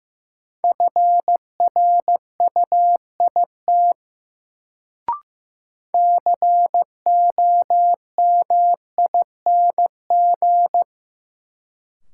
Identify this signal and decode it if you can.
Morse code, heard as a keyed tone, about 15 wpm, 700 Hz. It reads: FRUIT COMING
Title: Sentences from Top 1300 Words - New Word Drill - 15wpm